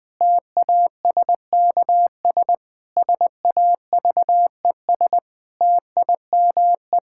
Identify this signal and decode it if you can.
Morse code, heard as a keyed tone, about 21 wpm, 700 Hz. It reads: TASKS SAVES TIME
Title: Difficult Sentences with Hints - 20wpm